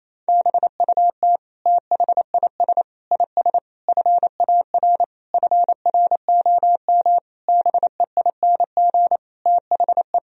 Morse code, heard as a keyed tone, about 28 wpm, 700 Hz. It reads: BUT THIS IS FAR FROM BEING THE
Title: Sentences from Top 1000 Words - Review Part 3 - 28wpm